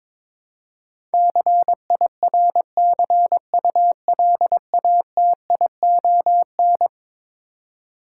Morse code, heard as a keyed tone, about 22 wpm, 700 Hz. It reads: CIRCULATION
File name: practice / 22wpm / Eleven Letter Words - 22wpm